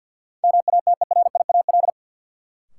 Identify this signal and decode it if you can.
Morse code, heard as a keyed tone, about 53 wpm, 700 Hz. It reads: MATERIAL
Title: Sentences from Top 1100 Words - New Word Drill Alt - 50wpm